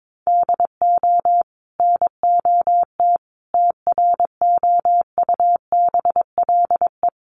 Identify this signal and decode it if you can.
Morse code, heard as a keyed tone, about 22 wpm, 700 Hz. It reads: DO NOT TROUBLE
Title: Sentences from Top 1200 Words 22wpm